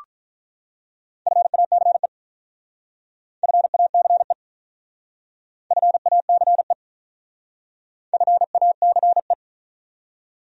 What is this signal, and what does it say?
Morse code, heard as a keyed tone, about 43 wpm, 700 Hz. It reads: FACE FACE FACE FACE
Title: Speed-Racing Top 500 Words 35wpm